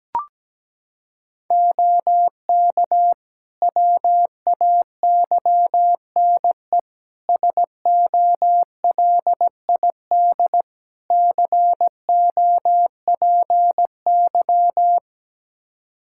Morse code, heard as a keyed tone, about 17 wpm, 700 Hz. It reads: OK WAYNE SOLID COPY
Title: Practice QSO - Exchange 3 - More Info - 17wpm